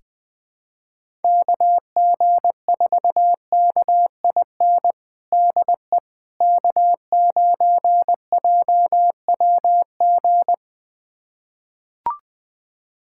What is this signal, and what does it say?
Morse code, heard as a keyed tone, about 20 wpm, 700 Hz. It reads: KG4KIN DE K9JWG
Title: USA Callsign DE Callsign 20wpm